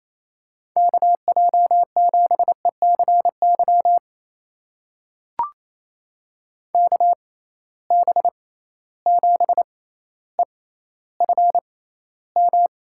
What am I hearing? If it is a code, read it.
Morse code, heard as a keyed tone, about 28 wpm, spaced as for 14 wpm, 700 Hz. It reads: KJ7ECY KB7EFM
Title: USA Call Signs - 6 Characters - 6x Character Spacing - 28wpm